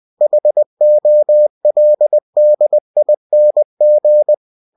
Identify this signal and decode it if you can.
Morse code, heard as a keyed tone, about 20 wpm, 600 Hz. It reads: HOLDING
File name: cw-books / winnie-the-pooh / winnie-20wpm-page-0025